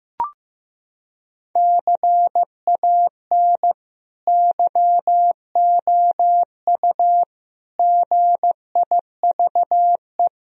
Morse code, heard as a keyed tone, about 15 wpm, 700 Hz. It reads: CAN YOU GIVE
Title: Sentences from Top 1000 Words - Review Part 3 - 15wpm